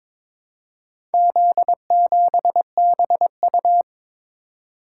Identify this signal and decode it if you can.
Morse code, heard as a keyed tone, about 22 wpm, 700 Hz. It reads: Z7BU
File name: practice / 22wpm / Four Letter-Number Combinations - 22wpm